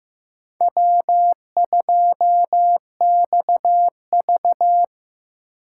Morse code, heard as a keyed tone, about 15 wpm, 700 Hz. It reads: W2XV
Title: USA Call Signs - 3x Character Spacing - 15wpm